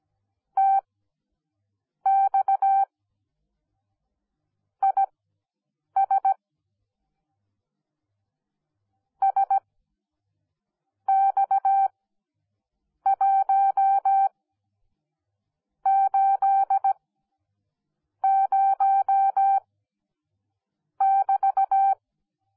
Morse code, heard as a keyed tone, about 17 wpm, spaced as for 6 wpm, 780 Hz. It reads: TX IS SX180<BT>